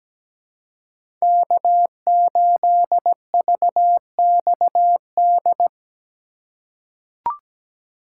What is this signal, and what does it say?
Morse code, heard as a keyed tone, about 17 wpm, 700 Hz. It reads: K8VXD
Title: USA Call Signs - 5 Characters - 4x Character Spacing - 17wpm